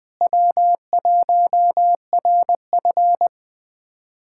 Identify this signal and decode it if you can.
Morse code, heard as a keyed tone, about 20 wpm, 700 Hz. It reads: W1RF
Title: USA Callsign DE Callsign - 4 Characters - 20wpm